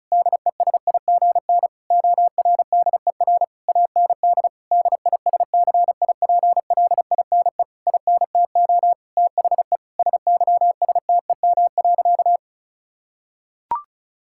Morse code, heard as a keyed tone, about 36 wpm, 690 Hz. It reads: DESIGN ORDER AND DISCIPLINE INTO THE SYSTEM.
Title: Sentences from Top 1000 Words - Review Part 6 - 35wpm